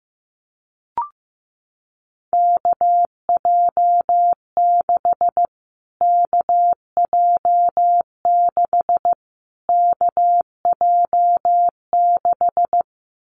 Morse code, heard as a keyed tone, about 15 wpm, 700 Hz. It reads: KJ6 KJ6 KJ6
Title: Callsign Prefixes for Top 25 DXCC - Repeated 3x - 15wpm